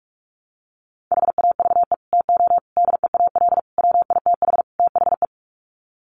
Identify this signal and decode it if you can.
Morse code, heard as a keyed tone, about 44 wpm, 700 Hz. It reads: HAVE TO DEAL WITH THE
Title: Sets of 5 Words Encore2 45wpm